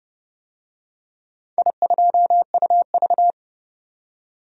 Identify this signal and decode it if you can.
Morse code, heard as a keyed tone, about 30 wpm, 700 Hz. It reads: I2UV
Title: DXCC-100 Call Signs - 4 Characters - 30wpm